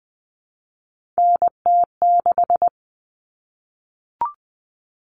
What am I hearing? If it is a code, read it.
Morse code, heard as a keyed tone, about 20 wpm, 700 Hz. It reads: NT6
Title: All Cut Numbers - 3-Digit 20wpm